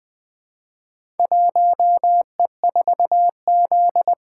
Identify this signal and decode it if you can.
Morse code, heard as a keyed tone, about 20 wpm, 700 Hz. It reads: 1E4Z
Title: Four Letter-Number Combinations - 20wpm